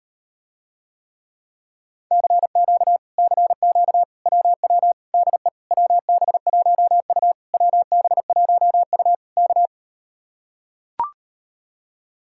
Morse code, heard as a keyed tone, about 38 wpm, 700 Hz. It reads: CQ CQ WW DE WB1U WB1U K